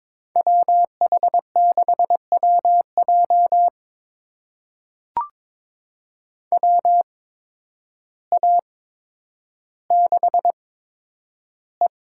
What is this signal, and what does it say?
Morse code, heard as a keyed tone, about 22 wpm, spaced as for 10 wpm, 700 Hz. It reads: WH6WJ WA6E
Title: USA Call Signs - 5 Characters - 8x Character Spacing - 22wpm